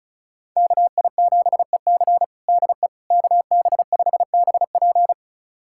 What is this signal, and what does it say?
Morse code, heard as a keyed tone, about 36 wpm, 700 Hz. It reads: KI7EC DE KB5BP
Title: USA Callsign DE Callsign - 5 Characters - 35wpm